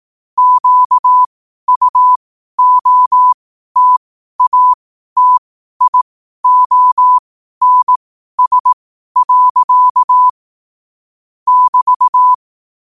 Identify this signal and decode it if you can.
Morse code, heard as a keyed tone, about 18 wpm, spaced as for 12 wpm, 1000 Hz. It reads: QUOTATIONS. <BT>
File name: qotdfiles / 12wpm